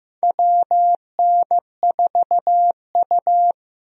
Morse code, heard as a keyed tone, about 15 wpm, 700 Hz. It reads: WN4U
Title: USA Call Signs - 4 Characters - Repeated 2x - 15wpm